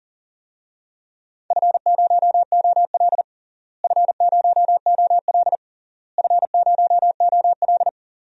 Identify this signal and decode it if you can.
Morse code, heard as a keyed tone, about 40 wpm, 700 Hz. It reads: F0OL F0OL F0OL
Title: DXCC-100 Call Signs - 4 Characters - Repeated 3x - 40wpm